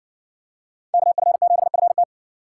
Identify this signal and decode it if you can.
Morse code, heard as a keyed tone, about 60 wpm, 700 Hz. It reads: KF7FT